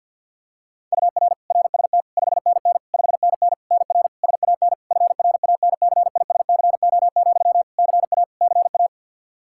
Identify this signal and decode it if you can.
Morse code, heard as a keyed tone, about 52 wpm, 700 Hz. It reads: UR RST 5NN 5NN NR SAN FRANCISCO, CA CA
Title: Practice QSO - Exchange 1 - Beginning - 50wpm